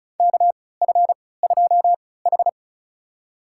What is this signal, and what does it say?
Morse code, heard as a keyed tone, about 34 wpm, spaced as for 20 wpm, 700 Hz. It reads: KF2H
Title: USA Call Signs - 4 Characters - 3x Character Spacing - 35wpm